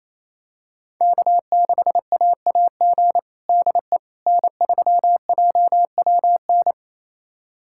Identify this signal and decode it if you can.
Morse code, heard as a keyed tone, about 28 wpm, 700 Hz. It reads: K6AAG DE N3JWN
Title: USA Callsign DE Callsign - 5 Characters - 28wpm